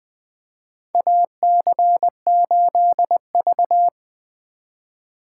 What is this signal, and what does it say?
Morse code, heard as a keyed tone, about 20 wpm, 700 Hz. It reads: AC8V
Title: USA Call Signs - 4 Characters - 3x Character Spacing - 20wpm